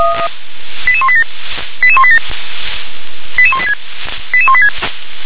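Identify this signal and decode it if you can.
Five-tone selective call (ZVEI), heard as quick runs of five tones, tones of about 70 ms. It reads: 88187 88187 88186 88186
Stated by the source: ZVEI five-tone selective call received by SDR.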